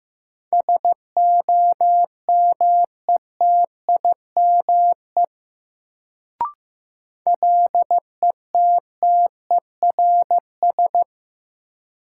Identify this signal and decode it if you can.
Morse code, heard as a keyed tone, about 15 wpm, 700 Hz. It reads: SOMETIME LETTERS
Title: Sentences from Top 1000 Words - Review Part 6 - New Word Drill - 15wpm